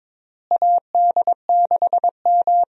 Morse code, heard as a keyed tone, about 22 wpm, 700 Hz. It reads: AD6M
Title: USA Call Signs - 4 Characters - Repeated 2x - 22wpm